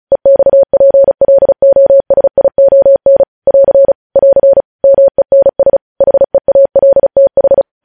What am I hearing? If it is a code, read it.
Morse code, heard as a keyed tone, about 36 wpm, 550 Hz. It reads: EXPLOSION <AR> <AR> MENS HEALTH
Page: News Headlines in Morse Code at 35 WPM